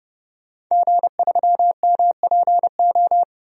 Morse code, heard as a keyed tone, about 30 wpm, 700 Hz. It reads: G3MPO
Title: DXCC-100 Call Signs - 5 Characters - 30wpm